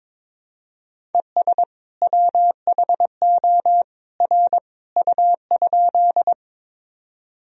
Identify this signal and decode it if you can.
Morse code, heard as a keyed tone, about 22 wpm, 700 Hz. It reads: ES WHO R U?
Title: Sentences from Top 100 Words and Abbreviations 22wpm